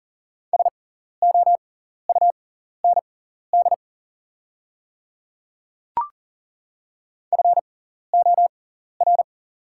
Morse code, heard as a keyed tone, about 40 wpm, spaced as for 14 wpm, 700 Hz. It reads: SOUND FOR
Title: 6x Character Spacing - Top 100 Words - 40wpm